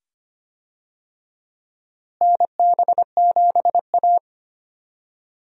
Morse code, heard as a keyed tone, about 25 wpm, 700 Hz. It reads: NB7A
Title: USA Call Signs - 4 Characters - Repeated 2x - 25wpm